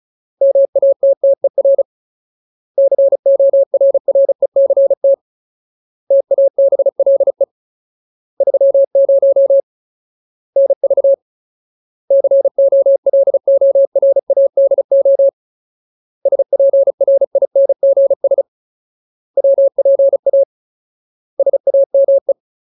Morse code, heard as a keyed tone, about 34 wpm, 550 Hz. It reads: MATTER CORRECT TABLE 30 NV COLORADO SPRINGS WPA SAME
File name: Ultimate Word List / 35WPM Ultimate